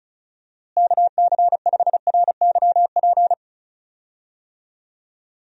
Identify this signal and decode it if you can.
Morse code, heard as a keyed tone, about 34 wpm, 700 Hz. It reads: KC5RYP
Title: USA Call Signs - 6 Characters - Repeated 2x - 35wpm